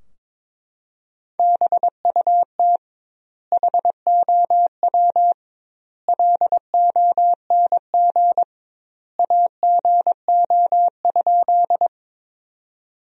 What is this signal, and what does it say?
Morse code, heard as a keyed tone, about 22 wpm, 700 Hz. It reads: BUT HOW LONG AGO?
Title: 2x Word-Spacing - Sentences from Top 500 Words - 22wpm